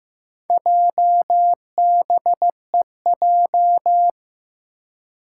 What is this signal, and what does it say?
Morse code, heard as a keyed tone, about 15 wpm, 700 Hz. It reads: JBEJ
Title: Four Challenging Letter-Number Combinations 15wpm